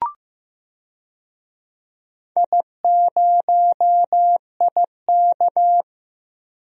Morse code, heard as a keyed tone, about 15 wpm, 700 Hz. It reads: I0IK
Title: DXCC-100 Call Signs - 4 Characters - Repeated 2x - 15wpm